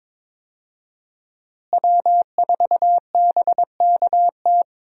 Morse code, heard as a keyed tone, about 22 wpm, 700 Hz. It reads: W4BKT
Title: USA Call Signs - 5 Characters - Repeated 2x - 22wpm